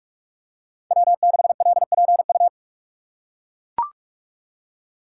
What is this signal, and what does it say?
Morse code, heard as a keyed tone, about 44 wpm, 690 Hz. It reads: W6LPU
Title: USA Call Signs - 5 Characters - 6x Character Spacing - 45wpm